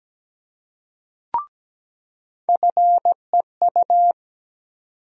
Morse code, heard as a keyed tone, about 17 wpm, 700 Hz. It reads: FEU
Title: USA Call Sign Postfixes - 3 Letters 17wpm